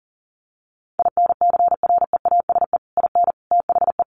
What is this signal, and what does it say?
Morse code, heard as a keyed tone, about 37 wpm, 700 Hz. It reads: INCREASE IN THE
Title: Sets of 5 Words 40wpm